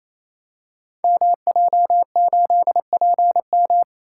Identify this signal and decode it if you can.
Morse code, heard as a keyed tone, about 27 wpm, 700 Hz. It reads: MJ8PM
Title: Code Groups - 5 Letter:Numbers - 28wpm